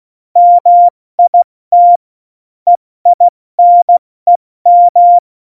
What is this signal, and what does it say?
Morse code, heard as a keyed tone, about 16 wpm, 700 Hz. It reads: MIT EINEM
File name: Morsekurs / Witze / Witz 29. Dezember 2020, 70 F80